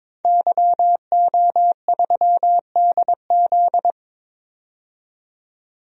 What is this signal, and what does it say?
Morse code, heard as a keyed tone, about 22 wpm, 700 Hz. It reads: YO3DZ
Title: DXCC-100 Call Signs - 5 Characters - 22wpm